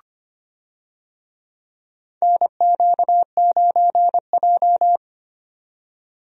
Morse code, heard as a keyed tone, about 25 wpm, 700 Hz. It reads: NQ9J
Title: USA Call Signs - 4 Characters - 25wpm